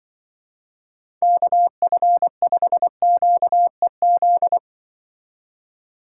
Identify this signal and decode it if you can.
Morse code, heard as a keyed tone, about 24 wpm, 700 Hz. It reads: KF5QEZ
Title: Speed Racing - USA 2x3 Call Signs 20wpm